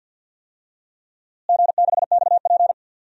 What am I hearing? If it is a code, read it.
Morse code, heard as a keyed tone, about 47 wpm, 700 Hz. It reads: G6XP
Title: DXCC-100 Call Signs - 4 Characters - Repeated 3x - 50wpm